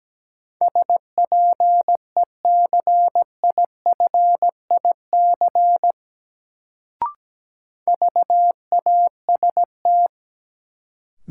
Morse code, heard as a keyed tone, about 17 wpm, 700 Hz. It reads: SPECIFIC VAST
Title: Sentences from Top 1300 Words - New Word Drill Alt - 17wpm